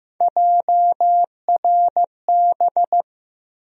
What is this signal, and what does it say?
Morse code, heard as a keyed tone, about 15 wpm, 700 Hz. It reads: JRB
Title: USA Call Sign Postfixes - 3 Letters 15wpm